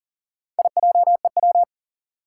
Morse code, heard as a keyed tone, about 39 wpm, 700 Hz. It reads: IJEW
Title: Four Challenging Letter-Number Combinations 40wpm